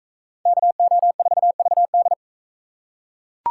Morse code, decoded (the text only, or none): KO4VD